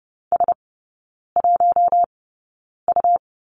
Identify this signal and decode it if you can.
Morse code, heard as a keyed tone, about 30 wpm, spaced as for 9 wpm, 690 Hz. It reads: S1U
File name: Lesson 17e - Letter u - Three Characters 30wpm